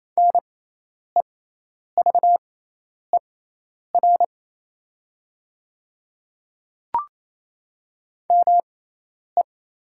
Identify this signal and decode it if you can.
Morse code, heard as a keyed tone, about 28 wpm, spaced as for 10 wpm, 690 Hz. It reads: NEVER ME
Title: 6x Character Spacing - Top 300 Words - 28wpm